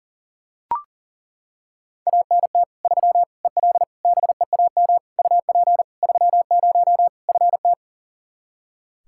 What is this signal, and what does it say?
Morse code, heard as a keyed tone, about 40 wpm, 700 Hz. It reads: ANT 3 EL BEAM UP 30 FT
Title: Practice QSO - Exchange 2 - Info - 40wpm